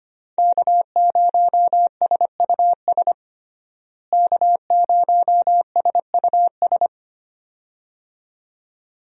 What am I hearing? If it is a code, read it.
Morse code, heard as a keyed tone, about 25 wpm, 700 Hz. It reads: K0SUS K0SUS
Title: USA Call Signs - 5 Characters - Repeated 2x - 25wpm